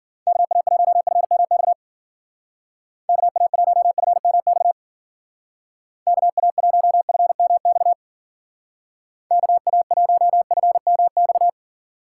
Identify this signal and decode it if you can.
Morse code, heard as a keyed tone, about 50 wpm, 700 Hz. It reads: KA1FMX KA1FMX KA1FMX KA1FMX
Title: Speed Racing - USA 2x3 Call Signs 40wpm